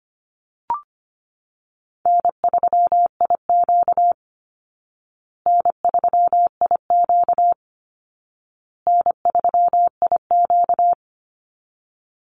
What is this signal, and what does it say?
Morse code, heard as a keyed tone, about 25 wpm, 700 Hz. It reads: N3IQ N3IQ N3IQ
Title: Call Signs Repeated 2x - 25wpm